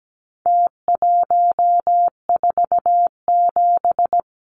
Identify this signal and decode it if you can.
Morse code, heard as a keyed tone, about 17 wpm, 700 Hz. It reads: T147
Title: Numbers - Contest Style - 4-Digit 17wpm